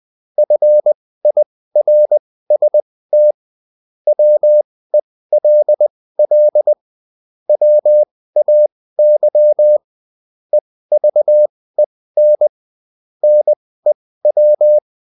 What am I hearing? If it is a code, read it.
Morse code, heard as a keyed tone, about 20 wpm, spaced as for 15 wpm, 600 Hz. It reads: FIRST WELL WAY EVEN NEW
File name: Top-100-Words-15